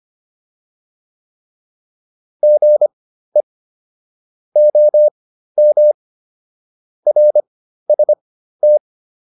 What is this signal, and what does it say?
Morse code, heard as a keyed tone, about 25 wpm, spaced as for 13 wpm, 600 Hz. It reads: GE OM RST